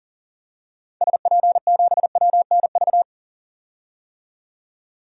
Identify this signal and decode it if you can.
Morse code, heard as a keyed tone, about 40 wpm, 700 Hz. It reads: SP7WNV
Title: DXCC-100 Call Signs - 6 Characters - 40wpm